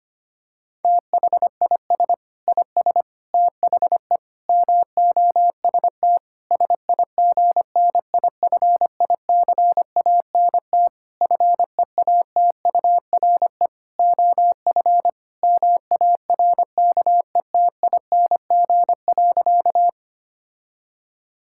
Morse code, heard as a keyed tone, about 25 wpm, 700 Hz. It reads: THIS IS THE MOST SIGNIFICANT FEATURE OF MARKETING.
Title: Sentences from Top 1000 Words - Review Part 6 - 25wpm